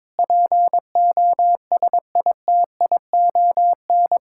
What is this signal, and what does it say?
Morse code, heard as a keyed tone, about 22 wpm, 700 Hz. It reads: POSITION
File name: practice / 22wpm / Sets of 7 Words - 22wpm